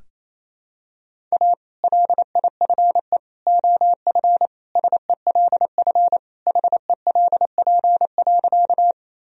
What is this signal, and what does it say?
Morse code, heard as a keyed tone, about 28 wpm, 700 Hz. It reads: A LIFE OF SELF HELP.
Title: Sentences from Top 1000 Words - Review Part 1 - 28wpm